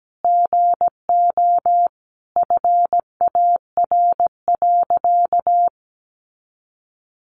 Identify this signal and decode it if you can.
Morse code, heard as a keyed tone, about 17 wpm, 700 Hz. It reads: GO FAR.
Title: Sentences from Top 400 Words 17wpm